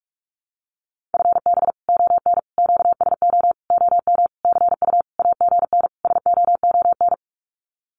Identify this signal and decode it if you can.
Morse code, heard as a keyed tone, about 44 wpm, 700 Hz. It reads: FB ON QSO OM CU AGN SOON
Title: Common QSO Phrases 45wpm